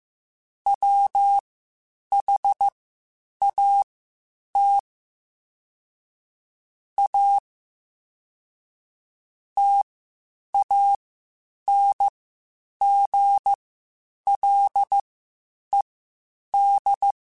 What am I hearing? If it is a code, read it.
Morse code, heard as a keyed tone, about 15 wpm, spaced as for 8 wpm, 800 Hz. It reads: WHAT A TANGLED
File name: Quotes Spaced / Quotes Spaced-mp3s / Deceive sp @13